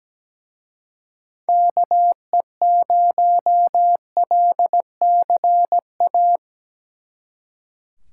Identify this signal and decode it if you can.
Morse code, heard as a keyed tone, about 20 wpm, 700 Hz. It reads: KE0LCA
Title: Speed Racing - USA Call Signs 17wpm